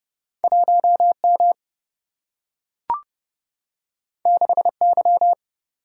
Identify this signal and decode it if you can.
Morse code, heard as a keyed tone, about 31 wpm, 700 Hz. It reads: 1 M 6 Y
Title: Single-Letter Single-Number 30wpm